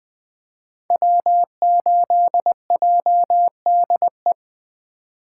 Morse code, heard as a keyed tone, about 20 wpm, 700 Hz. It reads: W8JDE
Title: 2023 POTA Activator Call Signs - 20wpm